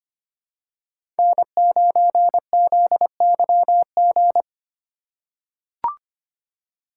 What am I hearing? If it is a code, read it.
Morse code, heard as a keyed tone, about 25 wpm, 700 Hz. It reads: N9ZYG
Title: USA Call Signs - 5 Characters - 2x Character Spacing - 25wpm